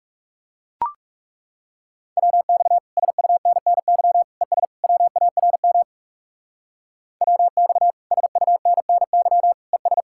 Morse code, heard as a keyed tone, about 37 wpm, 700 Hz. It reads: WX SUNNY ES WARM WX SUNNY ES
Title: Speed-Racing Common QSO Phrases - 30wpm